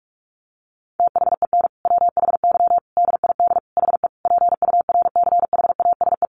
Morse code, heard as a keyed tone, about 44 wpm, 700 Hz. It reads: THEN WHY DID HE PURCHASE
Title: Sentences from Top 800 Words 45wpm